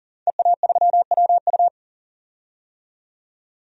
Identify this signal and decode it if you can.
Morse code, heard as a keyed tone, about 37 wpm, 700 Hz. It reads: EA3WU
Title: DXCC-100 Call Signs - 5 Characters - 40wpm